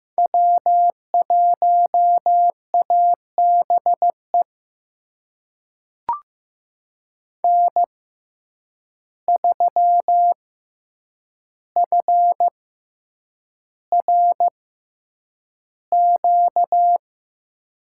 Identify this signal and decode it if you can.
Morse code, heard as a keyed tone, about 15 wpm, spaced as for 8 wpm, 700 Hz. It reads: W1ABE N3FRQ
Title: USA Call Signs - 6x Character Spacing - 15wpm